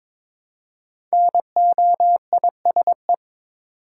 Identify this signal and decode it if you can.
Morse code, heard as a keyed tone, about 22 wpm, 690 Hz. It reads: NOISE
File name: 5 Letter Words - First 3 Letters Given - 22wpm